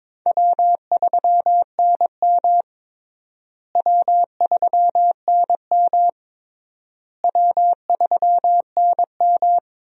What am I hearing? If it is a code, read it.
Morse code, heard as a keyed tone, about 22 wpm, 700 Hz. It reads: W3NM W3NM W3NM
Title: USA Call Signs - 4 Characters - Repeated 3x - 22wpm